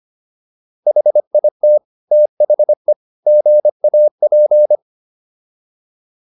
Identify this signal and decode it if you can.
Morse code, heard as a keyed tone, about 25 wpm, 600 Hz. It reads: HIT THE GAP